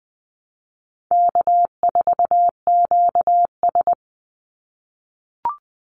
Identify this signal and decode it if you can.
Morse code, heard as a keyed tone, about 20 wpm, 700 Hz. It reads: K4QS
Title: NA SOTA Chaser Callsigns 20wpm